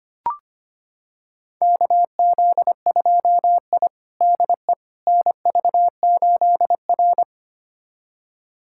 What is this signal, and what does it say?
Morse code, heard as a keyed tone, about 25 wpm, 700 Hz. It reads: KZ2I DE NV8R
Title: USA Callsign DE Callsign - 4 Characters - 25wpm